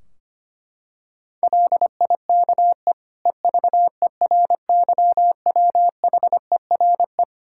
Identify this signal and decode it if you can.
Morse code, heard as a keyed tone, about 25 wpm, 700 Hz. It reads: LIKE EVERYWHERE
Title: Sentences from Top 1600 Words 25wpm